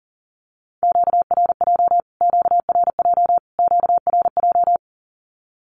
Morse code, heard as a keyed tone, about 40 wpm, 700 Hz. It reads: QRJ QRJ QRJ
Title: Q-Codes Repeated 3x 40wpm